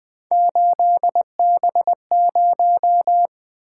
Morse code, heard as a keyed tone, about 24 wpm, 700 Hz. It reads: 8B0
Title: Speed-Racing - Three Letter-Number Combinations - 20wpm